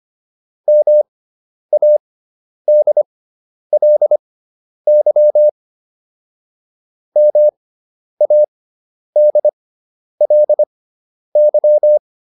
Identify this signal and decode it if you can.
Morse code, heard as a keyed tone, about 25 wpm, spaced as for 10 wpm, 600 Hz. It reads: MADLY MADLY